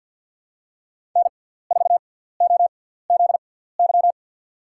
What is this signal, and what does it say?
Morse code, heard as a keyed tone, about 53 wpm, spaced as for 17 wpm, 700 Hz. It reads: N4OZY
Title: USA Call Signs - 5 Characters - 6x Character Spacing - 50wpm